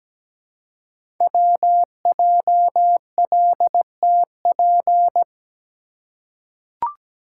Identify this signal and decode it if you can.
Morse code, heard as a keyed tone, about 17 wpm, 700 Hz. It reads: WJLTP
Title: Code Groups - 5 Letters - 17wpm